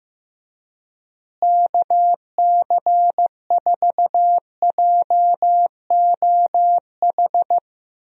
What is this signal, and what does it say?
Morse code, heard as a keyed tone, about 15 wpm, 700 Hz. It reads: KC4JOH
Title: USA Call Signs - 6 Characters - Repeated 2x - 15wpm